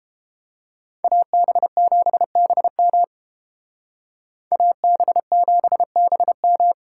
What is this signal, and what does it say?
Morse code, heard as a keyed tone, about 30 wpm, 700 Hz. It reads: AB7BM AB7BM